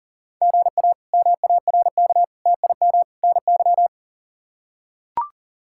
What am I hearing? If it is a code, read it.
Morse code, heard as a keyed tone, about 37 wpm, 700 Hz. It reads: GA MARK TIM NY
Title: K1USN SST QSO Practice - 40wpm